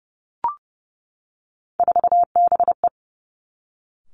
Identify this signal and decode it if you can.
Morse code, heard as a keyed tone, about 30 wpm, 700 Hz. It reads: 4BE
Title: Three Challenging Letter-Number Combinations 30wpm